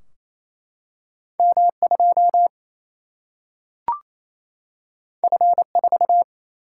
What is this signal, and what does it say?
Morse code, heard as a keyed tone, about 28 wpm, 700 Hz. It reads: M2 F4
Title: Single-Letter Single-Number 28wpm